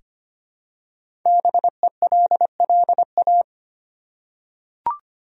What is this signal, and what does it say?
Morse code, heard as a keyed tone, about 25 wpm, 700 Hz. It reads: BELLA
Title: USA Common Names - 25wpm